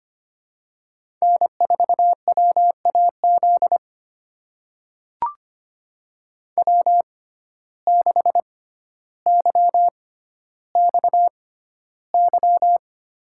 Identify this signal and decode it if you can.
Morse code, heard as a keyed tone, about 25 wpm, spaced as for 13 wpm, 700 Hz. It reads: N4WAZ W6YXY
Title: USA Call Signs - 6x Character Spacing - 25wpm